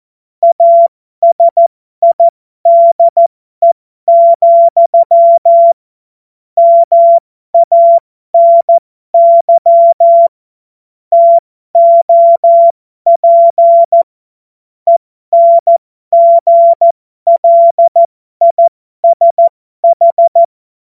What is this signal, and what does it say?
Morse code, heard as a keyed tone, about 14 wpm, spaced as for 12 wpm, 680 Hz. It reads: ASIDE, MANY TOP ENGLISH
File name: Morsekurs / Witze